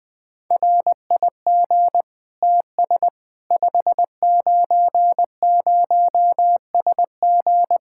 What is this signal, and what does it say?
Morse code, heard as a keyed tone, about 20 wpm, 700 Hz. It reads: RIG TS 590SG
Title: K1USN SST QSO Practice 20wpm